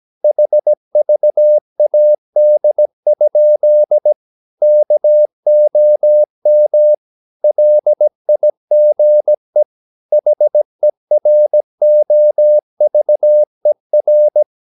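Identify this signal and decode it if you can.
Morse code, heard as a keyed tone, about 17 wpm, 600 Hz. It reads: HVAD? KOM LIGE HEROVER
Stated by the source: Slow (QRS) Morse practice at 17 wpm, from CW med Gnister.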